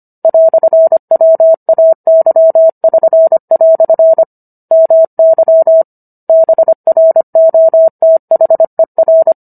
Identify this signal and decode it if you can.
Morse code, heard as a keyed tone, about 25 wpm, 650 Hz. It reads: "WAY<SN>" MY BROTHER
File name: Book One / 16-Chapter 16 - 25 WPM